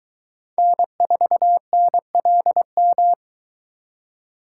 Morse code, heard as a keyed tone, about 22 wpm, 700 Hz. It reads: N4NLM